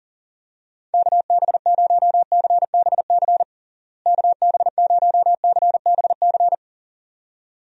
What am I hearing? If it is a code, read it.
Morse code, heard as a keyed tone, about 40 wpm, 700 Hz. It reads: KB0CBC KB0CBC
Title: USA Call Signs - 6 Characters - Repeated 2x - 40wpm